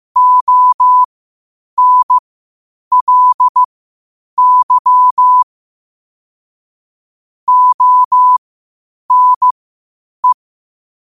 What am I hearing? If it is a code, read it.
Morse code, heard as a keyed tone, about 15 wpm, spaced as for 8 wpm, 1000 Hz. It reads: ONLY ONE
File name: qotdfiles / 8wpm